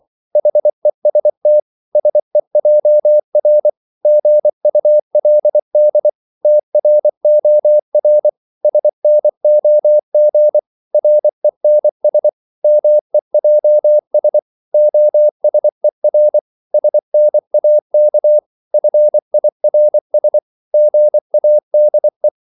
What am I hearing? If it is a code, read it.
Morse code, heard as a keyed tone, about 24 wpm, 600 Hz. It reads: HEST SEJR GULD TROR SNOG RENS MEJS OSER SNAK FIRS GADE
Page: Korte danske ord 24wpm